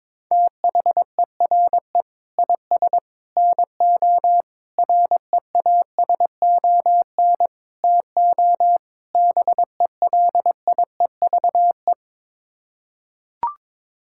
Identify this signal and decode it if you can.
Morse code, heard as a keyed tone, about 22 wpm, 700 Hz. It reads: THERE IS NO REASON TO BELIEVE
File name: practice / 22wpm / Sets of 6 Words - 22wpm